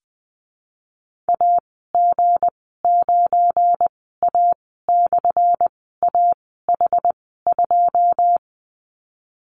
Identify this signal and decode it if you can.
Morse code, heard as a keyed tone, about 20 wpm, spaced as for 15 wpm, 700 Hz. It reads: AG9A/AH2
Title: 179 Lesson 29g - Slash - Call Signs 20wpm